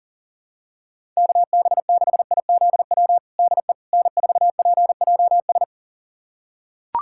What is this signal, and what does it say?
Morse code, heard as a keyed tone, about 40 wpm, 700 Hz. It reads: KB6IZW DE N4PJS
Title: USA Callsign DE Callsign 40wpm